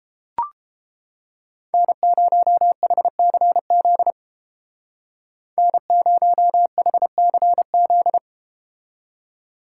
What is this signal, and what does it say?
Morse code, heard as a keyed tone, about 31 wpm, 700 Hz. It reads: N0HCZ N0HCZ